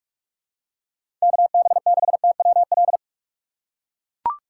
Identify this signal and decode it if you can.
Morse code, heard as a keyed tone, about 42 wpm, 700 Hz. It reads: KB6TWL
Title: USA Call Signs - 6 Characters - 3x Character Spacing - 45wpm